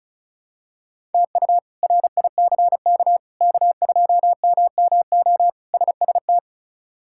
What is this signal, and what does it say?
Morse code, heard as a keyed tone, about 34 wpm, 700 Hz. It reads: TU RICK K2MMO SST
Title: K1USN SST QSO Practice - 35wpm